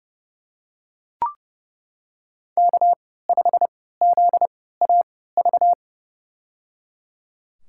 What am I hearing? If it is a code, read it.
Morse code, heard as a keyed tone, about 30 wpm, spaced as for 17 wpm, 700 Hz. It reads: K5ZAV
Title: USA Call Signs - 5 Characters - 3x Character Spacing - 30wpm